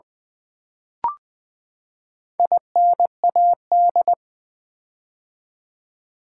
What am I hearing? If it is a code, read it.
Morse code, heard as a keyed tone, about 20 wpm, 700 Hz. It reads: INAD
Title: Build A Word - 4 Letters At A Time - 10-Letter Words - 20wpm